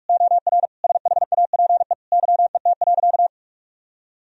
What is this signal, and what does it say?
Morse code, heard as a keyed tone, about 46 wpm, 700 Hz. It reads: OR SHAPE YET.
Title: Sentences from Top 1000 Words - Review Part 3 - 45wpm